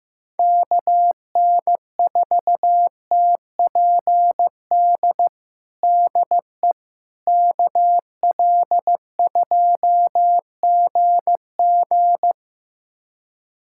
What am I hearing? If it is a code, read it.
Morse code, heard as a keyed tone, about 15 wpm, 700 Hz. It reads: KN4TPD DE KL2GG
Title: USA Callsign DE Callsign 15wpm